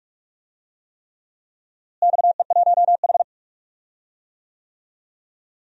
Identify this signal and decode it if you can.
Morse code, heard as a keyed tone, about 47 wpm, 700 Hz. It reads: XE1H
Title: DXCC-100 Call Signs - 4 Characters - Repeated 3x - 45wpm